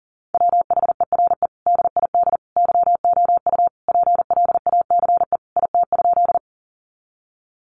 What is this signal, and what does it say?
Morse code, heard as a keyed tone, about 40 wpm, 700 Hz. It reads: WHERE DID YOU PLACE IT?
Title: Sentences from Top 200 Words 40wpm